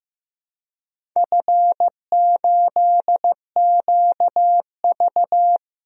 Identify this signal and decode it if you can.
Morse code, heard as a keyed tone, about 15 wpm, 700 Hz. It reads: F8QV
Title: DXCC-100 Call Signs - 4 Characters - Repeated 3x - 15wpm